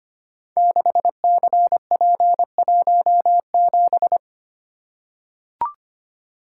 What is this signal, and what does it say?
Morse code, heard as a keyed tone, about 25 wpm, 700 Hz. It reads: 6CP17
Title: Code Groups - 5 Letter:Numbers - 25wpm